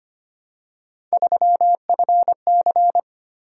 Morse code, heard as a keyed tone, about 25 wpm, 700 Hz. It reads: 3FC
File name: Three Letter-Number Combinations - 25wpm